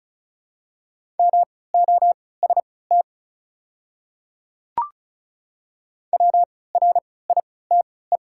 Morse code, heard as a keyed tone, about 34 wpm, spaced as for 20 wpm, 700 Hz. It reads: MOST WRITE
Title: 3x Character Spacing - Top 100 Words - 35wpm